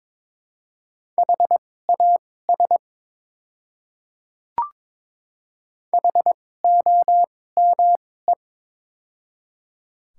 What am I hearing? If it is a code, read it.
Morse code, heard as a keyed tone, about 22 wpm, spaced as for 16 wpm, 700 Hz. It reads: HAS HOME
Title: 2x Character Spacing - Top 200 Words - 22wpm